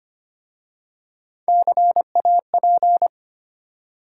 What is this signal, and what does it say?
Morse code, heard as a keyed tone, about 25 wpm, 690 Hz. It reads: CAP
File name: practice / 25wpm / Easy Words with Hints - 25wpm